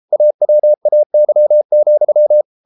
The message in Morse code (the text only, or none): AWAY,